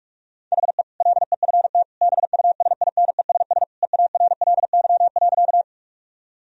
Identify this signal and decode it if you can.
Morse code, heard as a keyed tone, about 46 wpm, 700 Hz. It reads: HE LEFT BUSINESS EARLY.
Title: Sentences from Top 1000 Words - Review Part 2 - 45wpm